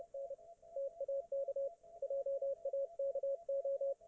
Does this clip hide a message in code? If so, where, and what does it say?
Yes: Morse code from 0.1 to 3.9 s, heard as a keyed tone, about 30 wpm, 550 Hz, with radio noise. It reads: N TAK JAKO